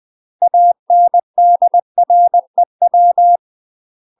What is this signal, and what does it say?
Morse code, heard as a keyed tone, about 20 wpm, 700 Hz. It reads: ANDREW
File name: UK MASTER QSOs TRAINING / 20WPM / UK MASTER QSOs 0057